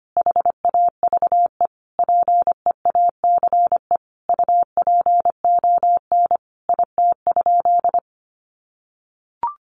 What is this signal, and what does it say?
Morse code, heard as a keyed tone, about 25 wpm, 700 Hz. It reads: HAVE PEACE UPON IT?
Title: Sentences from Top 1500 Words 25wpm